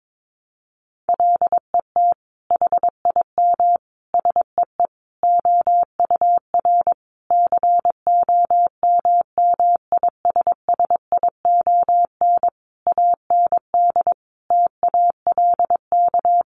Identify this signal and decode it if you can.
Morse code, heard as a keyed tone, about 22 wpm, 700 Hz. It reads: LET HIM SEE OUR COMMISSION AND TALK
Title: Sentences from Top 1400 Words 22wpm